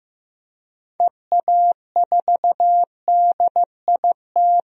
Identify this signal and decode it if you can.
Morse code, heard as a keyed tone, about 15 wpm, 700 Hz. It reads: EA4DIT
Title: DXCC-100 Call Signs - 6 Characters - 15wpm